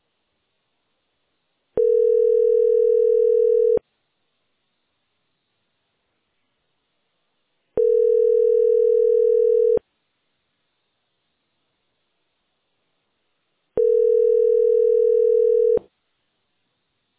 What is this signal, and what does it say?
Ringback tone (North America), 440 + 480 Hz.